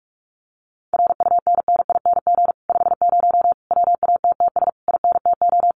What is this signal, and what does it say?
Morse code, heard as a keyed tone, about 46 wpm, 700 Hz. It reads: RUNNING 50 WATTS INTO
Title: Common QSO Phrases Encore 45wpm